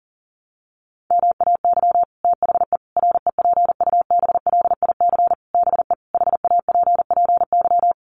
Morse code, heard as a keyed tone, about 37 wpm, 700 Hz. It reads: MAY THE REPUBLIC BE HAPPY
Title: Sentences from Top 1500 Words 40wpm